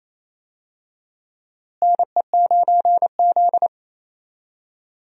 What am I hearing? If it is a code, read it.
Morse code, heard as a keyed tone, about 27 wpm, 700 Hz. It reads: NE9Z
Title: USA Call Signs - 4 Characters - Repeated 2x - 28wpm